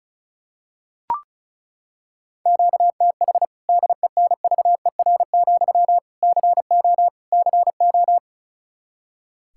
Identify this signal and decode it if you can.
Morse code, heard as a keyed tone, about 36 wpm, 700 Hz. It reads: QTH DENVER, CO CO
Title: K1USN SST QSO Practice 35wpm